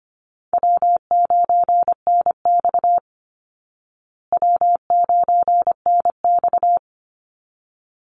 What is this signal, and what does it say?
Morse code, heard as a keyed tone, about 25 wpm, 700 Hz. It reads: W9NX W9NX
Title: Easy Call Signs Repeated 2x - 25wpm